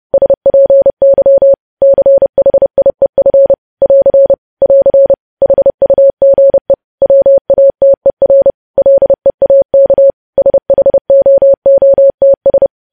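Morse code, heard as a keyed tone, about 30 wpm, 550 Hz. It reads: SPY CHIEF <AR> <AR> HUGE WATER LEAK SHOOTS
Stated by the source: Morse code news headlines sent at 30 WPM.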